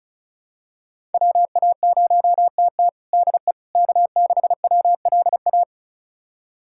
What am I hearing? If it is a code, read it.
Morse code, heard as a keyed tone, about 36 wpm, 700 Hz. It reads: WA0TT DE K6WLA
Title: USA Callsign DE Callsign - 5 Characters - 35wpm